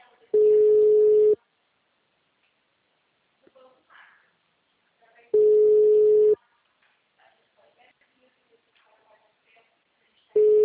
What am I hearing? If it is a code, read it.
Ringback tone (Europe, 425 Hz), with speech.